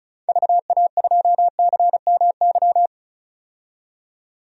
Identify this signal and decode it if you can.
Morse code, heard as a keyed tone, about 34 wpm, 700 Hz. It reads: VA2CMY
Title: DXCC-100 Call Signs - 6 Characters - 35wpm